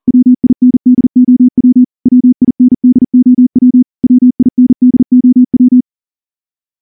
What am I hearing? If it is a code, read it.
Morse code, heard as a keyed tone, about 38 wpm, 250 Hz. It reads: WINDOW WINDOW WINDOW
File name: morse-training-40WPM